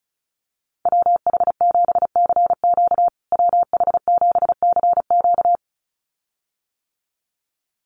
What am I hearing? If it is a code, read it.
Morse code, heard as a keyed tone, about 34 wpm, 700 Hz. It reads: WH7CQ WH7CQ
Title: USA Call Signs Repeated 2x 35wpm